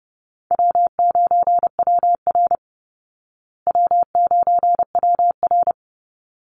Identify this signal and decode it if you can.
Morse code, heard as a keyed tone, about 30 wpm, 700 Hz. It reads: W9WR W9WR
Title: Call Signs Repeated 3x - 30wpm